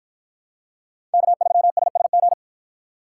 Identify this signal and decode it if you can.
Morse code, heard as a keyed tone, about 50 wpm, 700 Hz. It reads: K3SSG